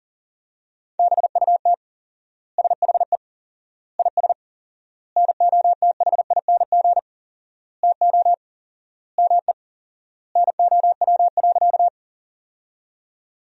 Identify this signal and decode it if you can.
Morse code, heard as a keyed tone, about 40 wpm, 700 Hz. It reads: BUT SHE IS NOTHING TO ME NOW.
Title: 4x Word-Spacing - Sentences from Top 500 Words - 40wpm